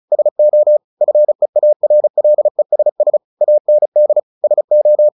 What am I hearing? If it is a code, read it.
Morse code, heard as a keyed tone, about 34 wpm, 600 Hz. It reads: SO FEARLESS AND SO
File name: the old man and the sea 2 35wpm